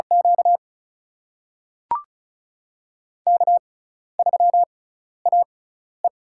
Morse code, heard as a keyed tone, about 36 wpm, spaced as for 12 wpm, 700 Hz. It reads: Q K3AE
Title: USA Call Signs - 4 Characters - 6x Character Spacing - 35wpm